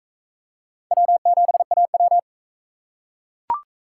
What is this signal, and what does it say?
Morse code, heard as a keyed tone, about 37 wpm, 700 Hz. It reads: W7AW